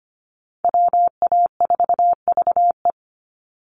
Morse code, heard as a keyed tone, about 25 wpm, 700 Hz. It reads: WA4VE
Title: USA Call Signs 25wpm